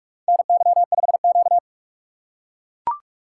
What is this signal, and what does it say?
Morse code, heard as a keyed tone, about 44 wpm, 690 Hz. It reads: NY5Q
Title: USA Call Signs - 4 Characters - 2x Character Spacing - 45wpm